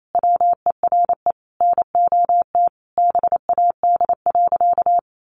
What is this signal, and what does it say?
Morse code, heard as a keyed tone, about 28 wpm, 700 Hz. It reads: WERE NOT BAD.
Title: Sentences from Top 900 Words 28wpm